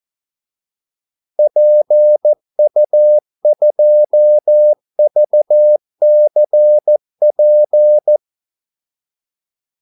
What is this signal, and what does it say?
Morse code, heard as a keyed tone, about 14 wpm, 600 Hz. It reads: PU2VCP
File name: Callsigns-14wpm